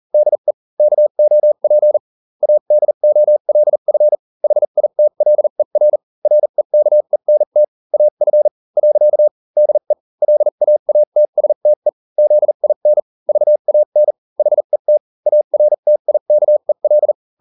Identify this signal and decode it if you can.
Morse code, heard as a keyed tone, about 40 wpm, spaced as for 34 wpm, 600 Hz. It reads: DE KOP ADOLF HITLER REKENT AF. DE LAATSTE ZIN VAN HET ARTIKEL